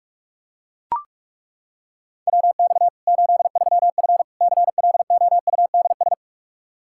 Morse code, heard as a keyed tone, about 41 wpm, 700 Hz. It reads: WX 83F CLOUDS
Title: Practice QSO - Exchange 2 - Info - 45wpm